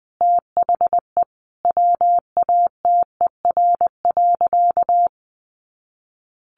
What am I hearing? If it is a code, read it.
Morse code, heard as a keyed tone, about 20 wpm, 700 Hz. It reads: THE WATER.
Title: Sentences from Top 500 Words 20wpm